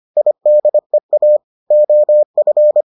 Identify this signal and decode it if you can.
Morse code, heard as a keyed tone, about 25 wpm, 600 Hz. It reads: IDEA OF